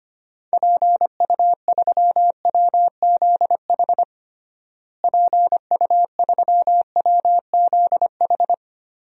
Morse code, heard as a keyed tone, about 25 wpm, 700 Hz. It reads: PU3WZH PU3WZH
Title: DXCC-100 Call Signs - 6 Characters - Repeated 2x - 25wpm